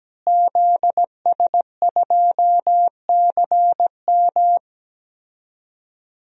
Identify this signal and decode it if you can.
Morse code, heard as a keyed tone, about 17 wpm, 700 Hz. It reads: ZS2CM
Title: DXCC-100 Call Signs - 5 Characters - Repeated 2x - 17wpm